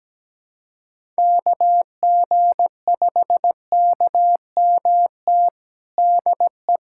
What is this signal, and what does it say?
Morse code, heard as a keyed tone, about 17 wpm, 700 Hz. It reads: KG5KMT DE
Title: USA Callsign DE Callsign - 6 Characters - 17wpm